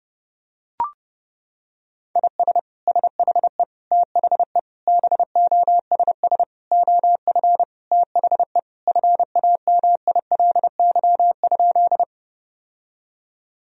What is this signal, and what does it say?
Morse code, heard as a keyed tone, about 32 wpm, 700 Hz. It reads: IS SHE THE BOSS OF THE FAMILY?
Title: Sentences from Top 1000 Words - Review Part 4 - 30wpm